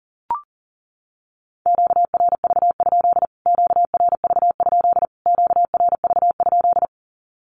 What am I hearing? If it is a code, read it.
Morse code, heard as a keyed tone, about 40 wpm, 700 Hz. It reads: QRV? QRV? QRV?
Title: Q-Codes Repeated 3x 40wpm